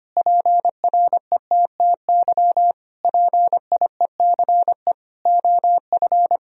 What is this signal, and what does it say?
Morse code, heard as a keyed tone, about 25 wpm, 700 Hz. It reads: PRETTY PIECE OF
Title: Sentences from Top 1000 Words - Review Part 2 - 25wpm